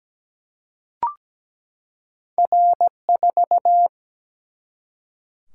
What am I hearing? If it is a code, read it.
Morse code, heard as a keyed tone, about 17 wpm, 700 Hz. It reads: R4
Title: Single-Letter Single-Number 17wpm